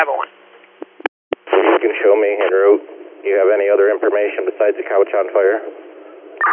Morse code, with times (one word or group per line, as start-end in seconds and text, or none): none